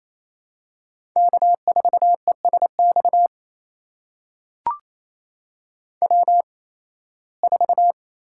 Morse code, heard as a keyed tone, about 28 wpm, spaced as for 18 wpm, 700 Hz. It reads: K4ESX W4
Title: USA Call Signs - 5 Characters - 8x Character Spacing - 28wpm